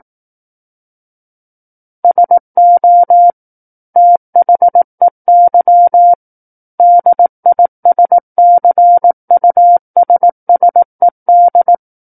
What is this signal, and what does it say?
Morse code, heard as a keyed tone, about 18 wpm, 700 Hz. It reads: SO THEY DISCUSSED
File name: 18 WPM